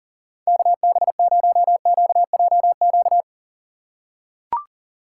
Morse code, heard as a keyed tone, about 37 wpm, 690 Hz. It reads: KB0QJQ
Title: USA Call Signs - 3x Character Spacing - 40wpm